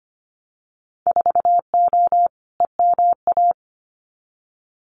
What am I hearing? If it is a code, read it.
Morse code, heard as a keyed tone, about 25 wpm, 700 Hz. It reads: 4O EMA
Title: Winter Field Day Exchanges Encore 25wpm